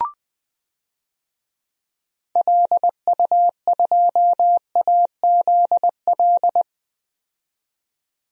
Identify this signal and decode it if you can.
Morse code, heard as a keyed tone, about 20 wpm, 700 Hz. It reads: LU2AZL
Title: DXCC-100 Call Signs - 6 Characters - 20wpm